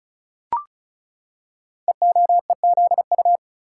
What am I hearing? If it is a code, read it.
Morse code, heard as a keyed tone, about 36 wpm, 700 Hz. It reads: EOEZU
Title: Code Groups - 5 Letters - 35wpm